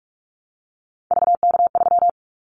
Morse code, heard as a keyed tone, about 46 wpm, 700 Hz. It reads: VK3
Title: Callsign Prefixes for Top 100 DXCC - 45wpm